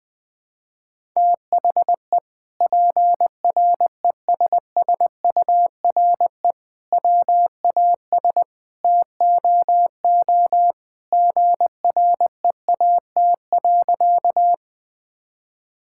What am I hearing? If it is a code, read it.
Morse code, heard as a keyed tone, about 21 wpm, 700 Hz. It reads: THE PRESSURE WAS TOO GREAT.
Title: Sentences from Top 1000 Words - Review Part 6 - 20wpm